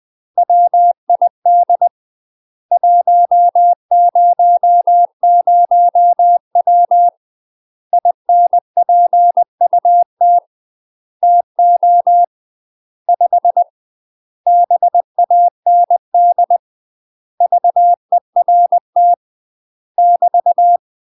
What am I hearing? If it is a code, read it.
Morse code, heard as a keyed tone, about 20 wpm, 700 Hz. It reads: WID 100W INPUT TO 5 BAND VERT <BT>
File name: UK MASTER QSOs TRAINING/20WPM